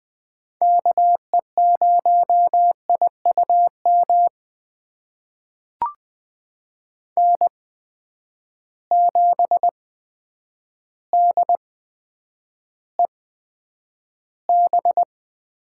Morse code, heard as a keyed tone, about 20 wpm, spaced as for 9 wpm, 700 Hz. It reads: KE0IUM N7DEB
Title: USA Call Signs - 8x Character Spacing - 20wpm